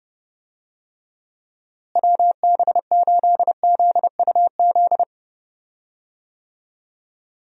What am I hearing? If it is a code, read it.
Morse code, heard as a keyed tone, about 30 wpm, 700 Hz. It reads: WB8ZUZ
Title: USA Call Signs - 6 Characters - Repeated 2x - 30wpm